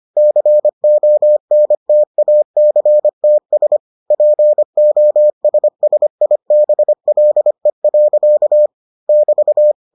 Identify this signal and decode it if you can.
Morse code, heard as a keyed tone, about 25 wpm, 600 Hz. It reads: CONTACTS POSSIBLE. <BT>